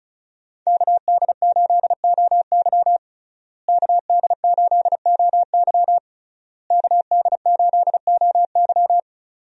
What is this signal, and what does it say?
Morse code, heard as a keyed tone, about 36 wpm, 700 Hz. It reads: KD8OY KD8OY KD8OY
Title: USA Call Signs - 5 Characters - Repeated 3x - 35wpm